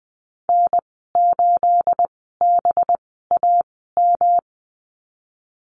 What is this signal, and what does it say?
Morse code, heard as a keyed tone, about 20 wpm, spaced as for 14 wpm, 700 Hz. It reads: N8BAM
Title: 219 Lesson 35g - Number 0 - Call Signs 20wpm